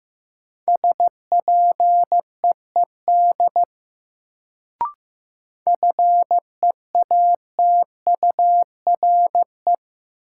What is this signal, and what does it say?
Morse code, heard as a keyed tone, about 15 wpm, 700 Hz. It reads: SPEED FEATURE
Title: Sentences from Top 800 Words - New Word Drill - 15wpm